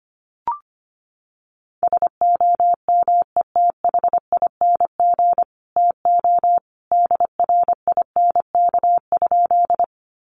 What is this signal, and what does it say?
Morse code, heard as a keyed tone, about 25 wpm, 700 Hz. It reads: SOMETHING TO DRINK?
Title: Sentences from Top 1500 Words 25wpm